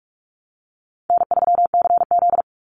Morse code, heard as a keyed tone, about 44 wpm, 700 Hz. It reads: N3CZ
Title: Call Signs Repeated 2x - 45wpm